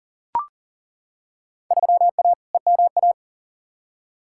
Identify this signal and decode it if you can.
Morse code, heard as a keyed tone, about 40 wpm, 700 Hz. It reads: 3A EMA
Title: Field Day Exchanges - 40wpm